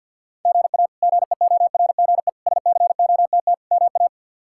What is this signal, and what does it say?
Morse code, heard as a keyed tone, about 52 wpm, 700 Hz. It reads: GA GEORGE SCOTT MA
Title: K1USN SST QSO Practice - 50wpm